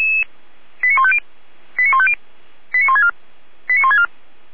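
Five-tone selective call (ZVEI), heard as quick runs of five tones, tones of about 70 ms. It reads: 89150 89150 89164 89164
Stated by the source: ZVEI five-tone selective call received on an SDR, from the feuerwehr-orlamuende.org fire brigade server.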